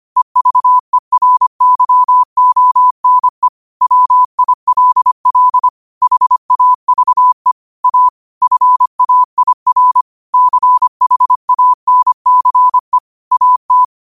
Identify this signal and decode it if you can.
Morse code, heard as a keyed tone, about 25 wpm, 1000 Hz. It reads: EVERYONE WILL HAVE A FAIR CHANCE AT